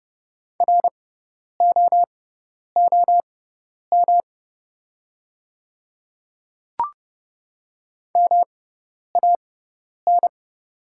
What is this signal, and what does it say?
Morse code, heard as a keyed tone, about 30 wpm, spaced as for 10 wpm, 700 Hz. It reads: ROOM MAN
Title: 6x Character Spacing - Top 300 Words - 30wpm